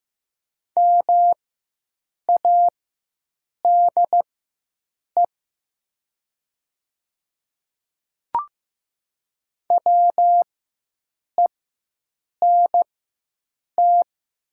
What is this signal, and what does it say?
Morse code, heard as a keyed tone, about 15 wpm, spaced as for 7 wpm, 700 Hz. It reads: MADE WENT
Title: 4x Character Spacing - Top 300 Words - 15wpm